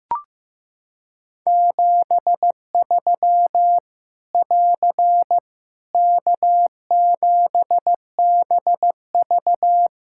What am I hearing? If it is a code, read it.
Morse code, heard as a keyed tone, about 15 wpm, 700 Hz. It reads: 73 <AR> K7BV
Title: Practice QSO - Exchange 4 - Ending - 15wpm